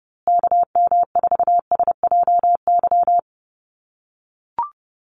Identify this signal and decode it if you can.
Morse code, heard as a keyed tone, about 30 wpm, 700 Hz. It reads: KM4SJY
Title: USA Call Signs Repeated 2x 30wpm